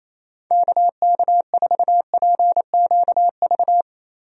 Morse code, heard as a keyed tone, about 28 wpm, 700 Hz. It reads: KK4PQV
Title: USA Call Signs - 6 Characters - Repeated 2x - 28wpm